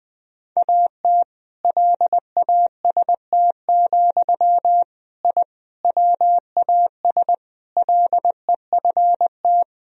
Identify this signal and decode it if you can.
Morse code, heard as a keyed tone, about 20 wpm, 700 Hz. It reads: AT LAST, I WAS LEFT
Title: Sentences from Top 1000 Words - Review Part 1 - 20wpm